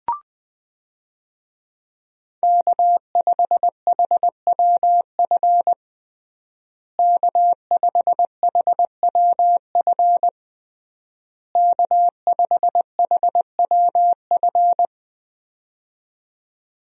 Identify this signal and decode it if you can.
Morse code, heard as a keyed tone, about 20 wpm, 700 Hz. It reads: K5HWF K5HWF K5HWF
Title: USA Call Signs - 5 Characters - Repeated 3x - 20wpm